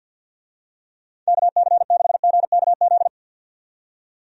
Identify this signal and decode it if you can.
Morse code, heard as a keyed tone, about 52 wpm, 700 Hz. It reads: KC6GKZ
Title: USA Call Signs - 6 Characters - 50wpm